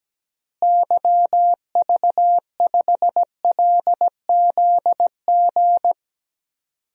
Morse code, heard as a keyed tone, about 17 wpm, 700 Hz. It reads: YV5LZG
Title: DXCC-100 Call Signs - 6 Characters - Repeated 3x - 17wpm